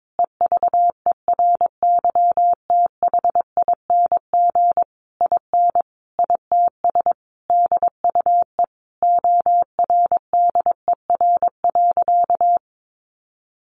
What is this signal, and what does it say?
Morse code, heard as a keyed tone, about 22 wpm, 700 Hz. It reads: EVERYTHING IN ITS DUE ORDER.
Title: Sentences from Top 900 Words 22wpm